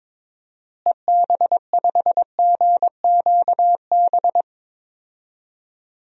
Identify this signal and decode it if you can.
Morse code, heard as a keyed tone, about 22 wpm, 700 Hz. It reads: EB5GQB
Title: DXCC-100 Call Signs - 6 Characters - Repeated 3x - 22wpm